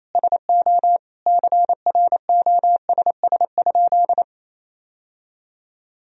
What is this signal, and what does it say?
Morse code, heard as a keyed tone, about 28 wpm, 700 Hz. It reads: SO CROSS?
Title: Sentences from Top 1000 Words - Review Part 1 - 28wpm